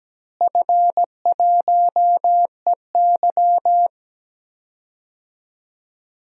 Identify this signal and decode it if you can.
Morse code, heard as a keyed tone, about 17 wpm, 700 Hz. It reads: F1EY
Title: DXCC-100 Call Signs - 4 Characters - Repeated 3x - 17wpm